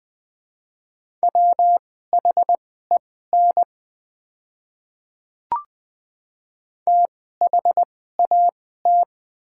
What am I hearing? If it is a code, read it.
Morse code, heard as a keyed tone, about 20 wpm, spaced as for 14 wpm, 700 Hz. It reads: WHEN THAT
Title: 2x Character Spacing - Top 300 Words - 20wpm